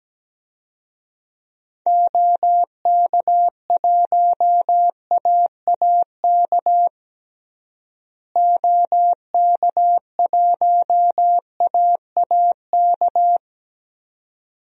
Morse code, heard as a keyed tone, about 17 wpm, 700 Hz. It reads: OK1AAK OK1AAK
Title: DXCC-100 Call Signs - 6 Characters - Repeated 2x - 17wpm